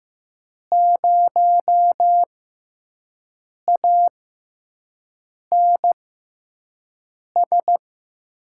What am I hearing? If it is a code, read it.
Morse code, heard as a keyed tone, about 15 wpm, spaced as for 5 wpm, 700 Hz. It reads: 0ANS
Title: USA Call Signs - 5 Characters - 6x Character Spacing - 15wpm